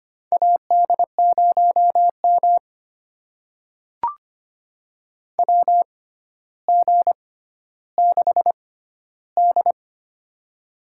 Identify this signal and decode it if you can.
Morse code, heard as a keyed tone, about 25 wpm, spaced as for 13 wpm, 700 Hz. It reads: AD0M WG6D
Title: USA Call Signs - 4 Characters - 6x Character Spacing - 25wpm